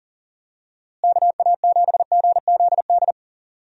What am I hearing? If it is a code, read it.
Morse code, heard as a keyed tone, about 40 wpm, 700 Hz. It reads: KA7GZD
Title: USA Call Signs - 6 Characters - Repeated 2x - 40wpm